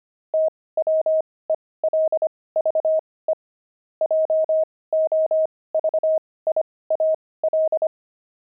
Morse code, heard as a keyed tone, about 25 wpm, spaced as for 18 wpm, 630 Hz. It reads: TWELVE JOVIAL
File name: pangram-fs-18-cs-25-friars